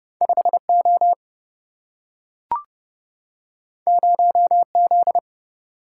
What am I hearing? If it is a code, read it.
Morse code, heard as a keyed tone, about 30 wpm, 700 Hz. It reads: 5 O 0 Z
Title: Single-Letter Single-Number 30wpm